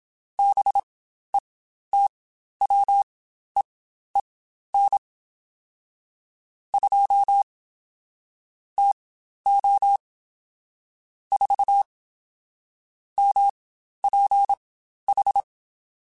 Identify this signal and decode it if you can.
Morse code, heard as a keyed tone, about 27 wpm, spaced as for 12 wpm, 800 Hz. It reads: BETWEEN 2 TO 4 MPH
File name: Spaced Characters / Spaced Characters-mp3s / Weather4 14 sp @25